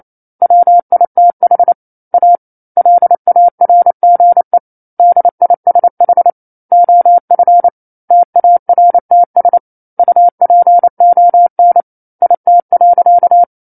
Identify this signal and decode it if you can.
Morse code, heard as a keyed tone, about 28 wpm, 700 Hz. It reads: WITH A LARGE DISH OF TARTS UPON IT.
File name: 09-Chapter-09 28 WPM